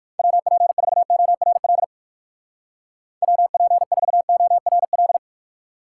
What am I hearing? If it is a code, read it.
Morse code, heard as a keyed tone, about 55 wpm, 700 Hz. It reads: WP4ORL WP4ORL
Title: Speed Racing - USA 2x3 Call Signs 45wpm